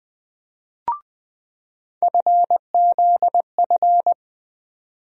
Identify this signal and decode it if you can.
Morse code, heard as a keyed tone, about 20 wpm, 700 Hz. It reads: FZF
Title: USA Call Sign Postfixes - 3 Letters 20wpm